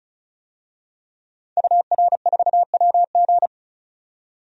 Morse code, heard as a keyed tone, about 36 wpm, 700 Hz. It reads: UR4WG
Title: DXCC-100 Call Signs - 5 Characters - 35wpm